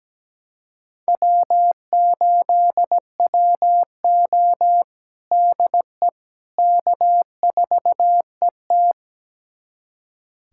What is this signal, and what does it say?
Morse code, heard as a keyed tone, about 17 wpm, 700 Hz. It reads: W8WO DE K4ET
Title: USA Callsign DE Callsign - 4 Characters - 17wpm